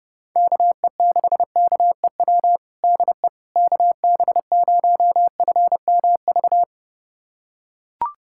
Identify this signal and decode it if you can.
Morse code, heard as a keyed tone, about 32 wpm, 700 Hz. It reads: KE6KEW DE KB0FMV
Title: USA Callsign DE Callsign 30wpm